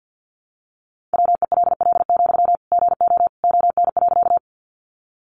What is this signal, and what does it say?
Morse code, heard as a keyed tone, about 50 wpm, 700 Hz. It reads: WELL, GO ON.
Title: Sentences from Top 200 Words 50wpm